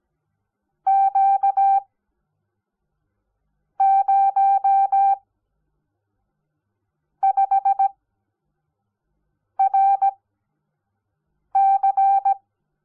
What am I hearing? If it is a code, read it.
Morse code, heard as a keyed tone, about 17 wpm, spaced as for 4 wpm, 780 Hz. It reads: Q05RC
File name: Onlinelehrgang / morsen / DJ4UF Morsen05